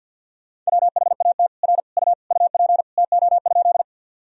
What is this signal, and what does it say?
Morse code, heard as a keyed tone, about 50 wpm, 700 Hz. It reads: WHAT R U UP TO?
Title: Sentences from Top 100 Words and Abbreviations 50wpm